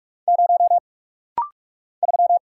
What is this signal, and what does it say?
Morse code, heard as a keyed tone, about 44 wpm, 700 Hz. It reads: 0 3
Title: Single Letter-Number Alt - Rapid-Fire 45wpm